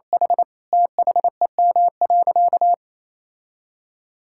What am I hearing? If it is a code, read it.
Morse code, heard as a keyed tone, about 27 wpm, 700 Hz. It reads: H THEM.
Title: Sentences from Top 1000 Words - Review Part 5 - 28wpm